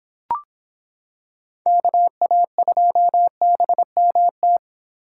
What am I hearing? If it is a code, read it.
Morse code, heard as a keyed tone, about 26 wpm, 700 Hz. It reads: KA2BMT